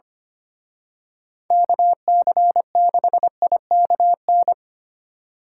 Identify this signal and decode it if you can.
Morse code, heard as a keyed tone, about 31 wpm, 700 Hz. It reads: KC6IKN
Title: Speed Racing - USA Call Signs 25wpm